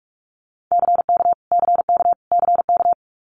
Morse code, heard as a keyed tone, about 46 wpm, 700 Hz. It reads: CK CK CK
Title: Multi-letter Phonemes Repeated 3x 45wpm